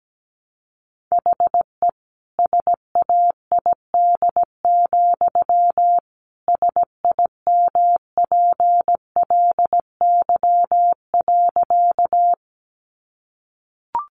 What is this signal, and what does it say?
Morse code, heard as a keyed tone, about 17 wpm, 700 Hz. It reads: HE SAID, SIMPLY.
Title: Sentences from Top 600 Words 17wpm